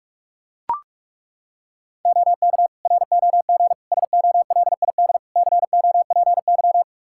Morse code, heard as a keyed tone, about 41 wpm, 700 Hz. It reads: OK ROG SOLID COPY
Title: Practice QSO - Exchange 3 - More Info - 45wpm